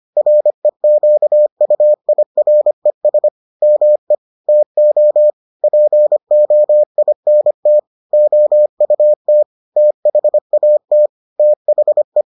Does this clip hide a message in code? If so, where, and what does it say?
Yes: Morse code from 0.2 to 12.2 s, heard as a keyed tone, about 25 wpm, 600 Hz. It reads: REQUIRES ME TO POINT OUT THAT THE